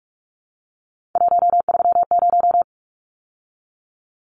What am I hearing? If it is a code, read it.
Morse code, heard as a keyed tone, about 44 wpm, 700 Hz. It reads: 130
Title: Numbers - 3-Digit 45wpm